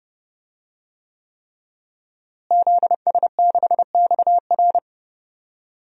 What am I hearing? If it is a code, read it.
Morse code, heard as a keyed tone, about 30 wpm, 700 Hz. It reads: ZS6XR
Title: DXCC-100 Call Signs - 5 Characters - Repeated 3x - 30wpm